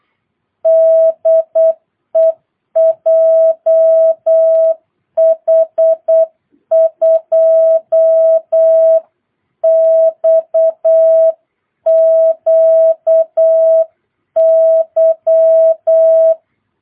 Morse code, heard as a keyed tone, about 8 wpm, 650 Hz. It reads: DE JH2XQY
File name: jh8cbh / 2022cw